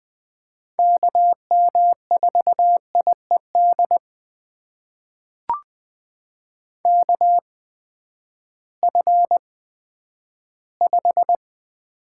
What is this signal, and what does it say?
Morse code, heard as a keyed tone, about 21 wpm, spaced as for 11 wpm, 700 Hz. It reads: KM4IED KF5
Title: USA Call Signs - 6 Characters - 8x Character Spacing - 20wpm